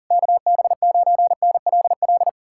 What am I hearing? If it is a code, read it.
Morse code, heard as a keyed tone, about 52 wpm, 700 Hz. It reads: KB9NLL